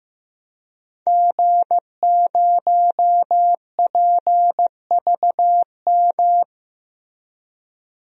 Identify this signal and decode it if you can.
Morse code, heard as a keyed tone, about 16 wpm, 700 Hz. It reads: G0PVM
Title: DXCC-100 Call Signs - 5 Characters - 15wpm